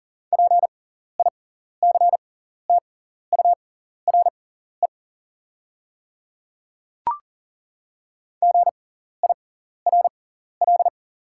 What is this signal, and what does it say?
Morse code, heard as a keyed tone, about 40 wpm, spaced as for 14 wpm, 690 Hz. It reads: PICTURE GIRL
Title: 6x Character Spacing - Top 500 Words - 40wpm